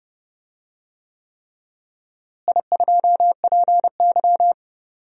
Morse code, heard as a keyed tone, about 30 wpm, 700 Hz. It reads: I2PY
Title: DXCC-100 Call Signs - 4 Characters - Repeated 3x - 30wpm